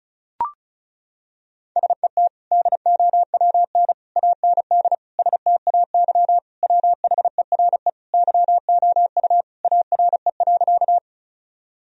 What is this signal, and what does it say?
Morse code, heard as a keyed tone, about 36 wpm, 700 Hz. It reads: SET DOWN AND STAY WHERE YOU ARE.
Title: Sentences from Top 1000 Words - Review Part 1 - 35wpm